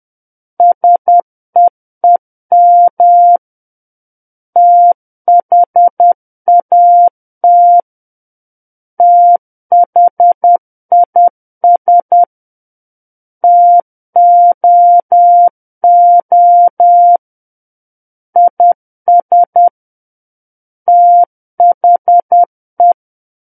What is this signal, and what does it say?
Morse code, heard as a keyed tone, about 10 wpm, 700 Hz. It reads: SEEM THAT THIS TOO IS THE
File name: Book One / 01-Chapter 1 - 10 WPM